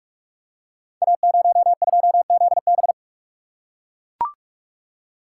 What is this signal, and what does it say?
Morse code, heard as a keyed tone, about 44 wpm, 700 Hz. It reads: A02ZB
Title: Code Groups - 5 Letter:Numbers - 45wpm